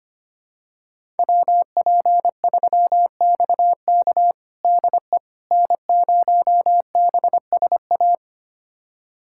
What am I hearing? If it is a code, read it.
Morse code, heard as a keyed tone, about 25 wpm, 700 Hz. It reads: WP3XK DE N0BSA
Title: USA Callsign DE Callsign - 5 Characters - 25wpm